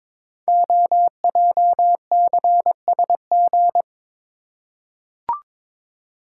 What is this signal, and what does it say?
Morse code, heard as a keyed tone, about 22 wpm, 700 Hz. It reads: OJCSG
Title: Code Groups - 5 Letters - 22wpm